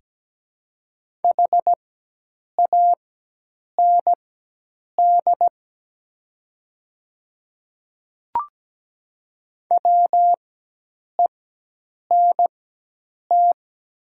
Morse code, heard as a keyed tone, about 17 wpm, spaced as for 8 wpm, 700 Hz. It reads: HAND WENT
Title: 4x Character Spacing - Top 300 Words - 17wpm